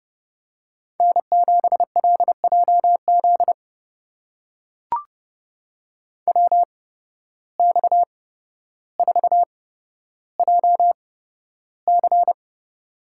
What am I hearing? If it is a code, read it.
Morse code, heard as a keyed tone, about 30 wpm, spaced as for 13 wpm, 700 Hz. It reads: N7LJZ WX4JC
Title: USA Call Signs - 5 Characters - 8x Character Spacing - 30wpm